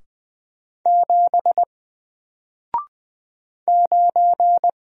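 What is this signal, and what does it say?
Morse code, heard as a keyed tone, about 20 wpm, 700 Hz. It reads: 7 9
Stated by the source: Single letter and number Morse drill at 20 wpm, from N4LCD.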